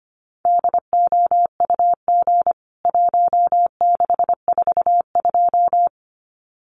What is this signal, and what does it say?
Morse code, heard as a keyed tone, about 25 wpm, 700 Hz. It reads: DOUG 1642
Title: CWT Exchange 25wpm